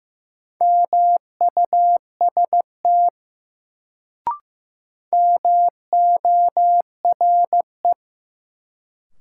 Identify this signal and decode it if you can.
Morse code, heard as a keyed tone, about 15 wpm, 700 Hz. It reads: MUST MORE
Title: Sets of 2 Words - New Word Drill - 15wpm